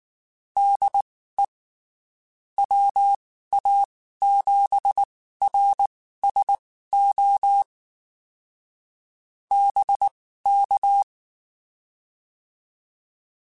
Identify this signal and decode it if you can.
Morse code, heard as a keyed tone, about 19 wpm, spaced as for 14 wpm, 800 Hz. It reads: DE WA7RSO BK
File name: (spaced) / QSOs Spaced-mp3s / WD0BGP 1 @18 sp